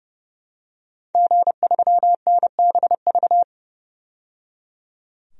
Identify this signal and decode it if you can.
Morse code, heard as a keyed tone, about 31 wpm, 700 Hz. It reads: G3NBV
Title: DXCC-100 Call Signs - 5 Characters - 30wpm